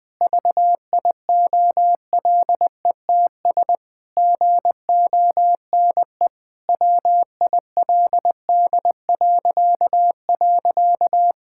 Morse code, heard as a keyed tone, about 21 wpm, 700 Hz. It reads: VIOLETS GONE WILD..
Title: Pangram Challenge v2 - 20wpm